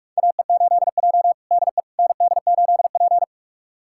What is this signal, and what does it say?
Morse code, heard as a keyed tone, about 44 wpm, 690 Hz. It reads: AE8J DE ND8P
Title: USA Callsign DE Callsign - 4 Characters - 45wpm